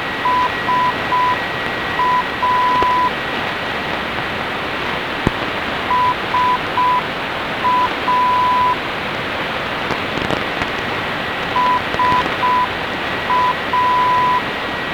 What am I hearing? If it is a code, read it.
Morse code, heard as a keyed tone, about 6 wpm, 1000 Hz. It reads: SA SA SA